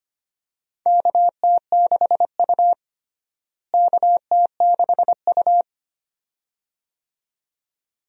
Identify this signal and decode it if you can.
Morse code, heard as a keyed tone, about 25 wpm, 700 Hz. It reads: KT6U KT6U
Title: USA Call Signs - 4 Characters - Repeated 2x - 25wpm